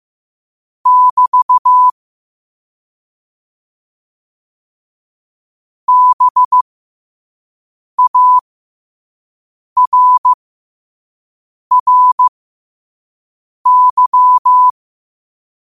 Morse code, heard as a keyed tone, about 15 wpm, spaced as for 5 wpm, 1000 Hz. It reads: <BT> BARRY
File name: qotdfiles / 5wpm